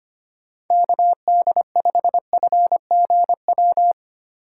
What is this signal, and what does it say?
Morse code, heard as a keyed tone, about 25 wpm, 700 Hz. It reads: KD5FGW
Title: USA Call Signs - 4x Character Spacing - 25wpm